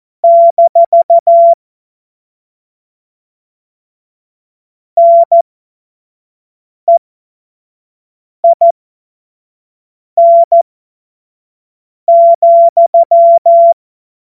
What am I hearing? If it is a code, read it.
Morse code, heard as a keyed tone, about 14 wpm, spaced as for 5 wpm, 680 Hz. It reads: - NEIN,
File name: Morsekurs / Witze / Witz 11. Juli 2023 25 F70